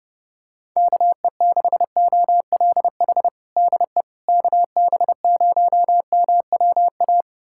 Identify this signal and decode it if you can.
Morse code, heard as a keyed tone, about 30 wpm, 700 Hz. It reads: KE6OLH DE KB0MWA
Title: USA Callsign DE Callsign - 6 Characters - 30wpm